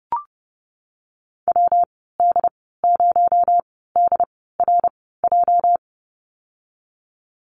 Morse code, heard as a keyed tone, about 30 wpm, spaced as for 17 wpm, 700 Hz. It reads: WD0DRJ
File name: Lesson 37g - Letter x - Call Signs 30wpm